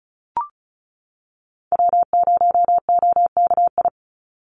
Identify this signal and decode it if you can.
Morse code, heard as a keyed tone, about 34 wpm, 700 Hz. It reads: W0OKI
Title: USA Call Signs 35wpm